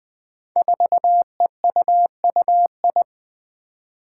Morse code, heard as a keyed tone, about 20 wpm, 700 Hz. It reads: 4EUUI
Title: Five Challenging Letter-Number Combinations 20wpm